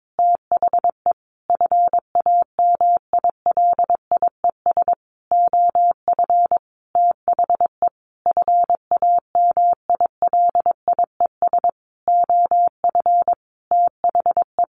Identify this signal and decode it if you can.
Morse code, heard as a keyed tone, about 22 wpm, 700 Hz. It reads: THE FAMILIES OF THE FAMILIES OF THE
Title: A Sound Of Thunder 22wpm